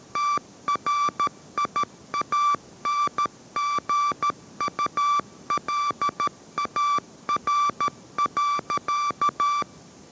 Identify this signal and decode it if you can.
Morse code, heard as a keyed tone, about 13 wpm, 1160 Hz. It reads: TRIANGULAR.